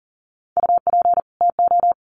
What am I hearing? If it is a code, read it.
Morse code, heard as a keyed tone, about 40 wpm, 700 Hz. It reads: UP TO
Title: Sets of 2 Words 40wpm